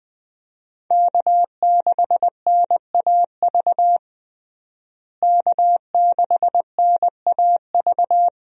Morse code, heard as a keyed tone, about 21 wpm, 700 Hz. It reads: K6NAV K6NAV
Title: USA Call Signs - 5 Characters - Repeated 3x - 20wpm